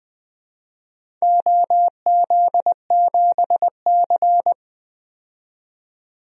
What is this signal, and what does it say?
Morse code, heard as a keyed tone, about 20 wpm, 700 Hz. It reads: OZ7C
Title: DXCC-100 Call Signs - 4 Characters - 20wpm